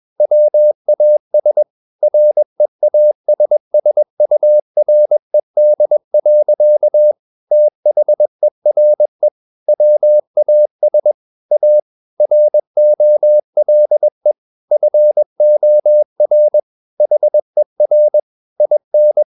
Morse code, heard as a keyed tone, about 21 wpm, 600 Hz. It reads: WAS REASSURED. THERE WAS A ROLE FOR HER IN
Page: CW med Gnister